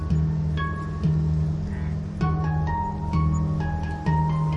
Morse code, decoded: none